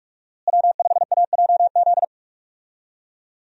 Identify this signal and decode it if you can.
Morse code, heard as a keyed tone, about 44 wpm, 700 Hz. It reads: W5AJZ
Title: USA Call Signs - 5 Characters - Repeated 3x - 45wpm